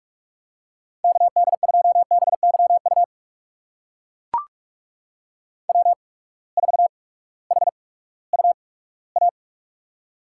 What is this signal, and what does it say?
Morse code, heard as a keyed tone, about 44 wpm, spaced as for 20 wpm, 700 Hz. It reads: KD2BYU W4HUA
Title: USA Call Signs - 8x Character Spacing - 45wpm